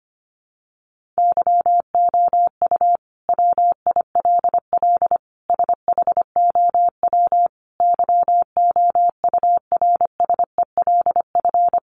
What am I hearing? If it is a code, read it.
Morse code, heard as a keyed tone, about 25 wpm, 700 Hz. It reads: YOU WILL SHOW YOURSELF
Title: Sentences from Top 600 Words 25wpm